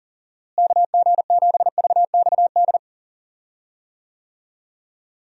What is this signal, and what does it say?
Morse code, heard as a keyed tone, about 38 wpm, 700 Hz. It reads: KG7VXD
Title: USA Call Signs - 6 Characters - Repeated 3x - 40wpm